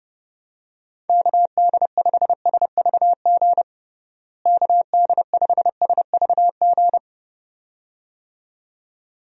Morse code, heard as a keyed tone, about 30 wpm, 700 Hz. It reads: KD5SVG KD5SVG
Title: USA Call Signs - 6 Characters - Repeated 2x - 30wpm